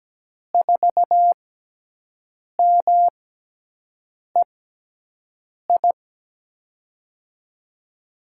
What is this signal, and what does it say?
Morse code, heard as a keyed tone, about 17 wpm, spaced as for 6 wpm, 700 Hz. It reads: 4MEI
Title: USA Call Signs - 6 Characters - 6x Character Spacing - 17wpm